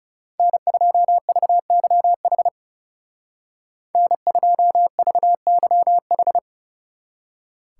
Morse code, heard as a keyed tone, about 36 wpm, 700 Hz. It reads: N2VYH N2VYH
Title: Speed Racing - USA 1x3 Call Signs 30wpm